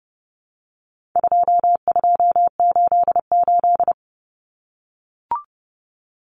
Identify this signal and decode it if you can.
Morse code, heard as a keyed tone, about 30 wpm, 700 Hz. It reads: 2288
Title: Common Cut Numbers - 4-Digit 30wpm